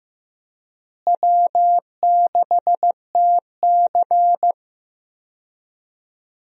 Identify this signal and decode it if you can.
Morse code, heard as a keyed tone, about 15 wpm, 700 Hz. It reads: W6TC
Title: USA Call Signs - 4 Characters - Repeated 2x - 15wpm